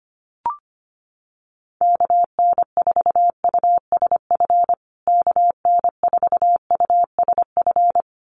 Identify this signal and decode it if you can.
Morse code, heard as a keyed tone, about 25 wpm, 700 Hz. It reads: KN4USF KN4USF
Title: USA Call Signs Repeated 2x 25wpm